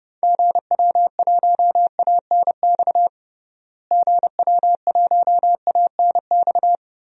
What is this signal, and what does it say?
Morse code, heard as a keyed tone, about 30 wpm, 700 Hz. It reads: GW1ANX GW1ANX
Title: DXCC-100 Call Signs - 6 Characters - Repeated 3x - 30wpm